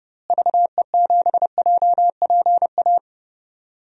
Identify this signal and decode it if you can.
Morse code, heard as a keyed tone, about 30 wpm, 700 Hz. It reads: VE7JPA
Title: DXCC-100 Call Signs - 6 Characters - Repeated 2x - 30wpm